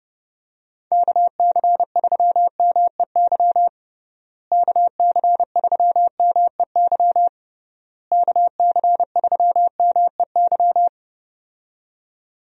Morse code, heard as a keyed tone, about 30 wpm, 700 Hz. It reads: KC3MEY KC3MEY KC3MEY
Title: USA Call Signs - 6 Characters - Repeated 3x - 30wpm